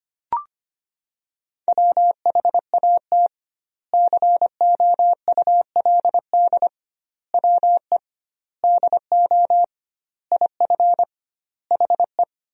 Morse code, heard as a keyed tone, about 25 wpm, 700 Hz. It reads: WHAT COULD WE DO IF HE
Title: 2x Word-Spacing - Sentences from Top 400 Words - 25wpm